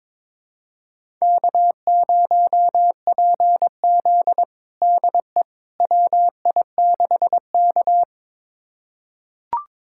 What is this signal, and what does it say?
Morse code, heard as a keyed tone, about 22 wpm, 700 Hz. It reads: K0PZ DE WI6K
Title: USA Callsign DE Callsign - 4 Characters - 22wpm